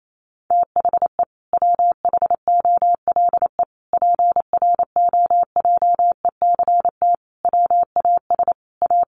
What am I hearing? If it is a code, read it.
Morse code, heard as a keyed tone, about 28 wpm, 700 Hz. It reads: THE WHOLE PROJECT WAS A
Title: Sentences from Top 1400 Words 28wpm